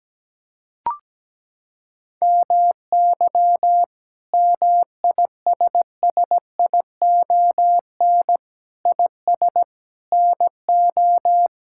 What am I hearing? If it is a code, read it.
Morse code, heard as a keyed tone, about 17 wpm, 700 Hz. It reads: MY MISSION IS NO
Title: Sentences from Top 1600 Words 17wpm